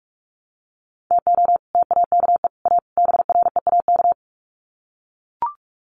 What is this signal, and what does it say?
Morse code, heard as a keyed tone, about 44 wpm, 700 Hz. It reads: TO TAKE A BREAK
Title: Sets of 4 Words 45wpm